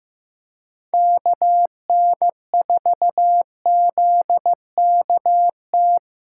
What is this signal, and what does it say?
Morse code, heard as a keyed tone, about 16 wpm, 700 Hz. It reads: KN4ZKT
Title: K1USN SST QSO Practice - 15wpm